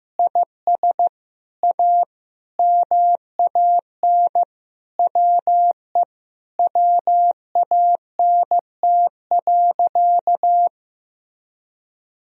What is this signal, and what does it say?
Morse code, heard as a keyed tone, about 15 wpm, 700 Hz. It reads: IS A MAN WE WANT.
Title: Sentences from Top 200 Words and Abbreviations 15wpm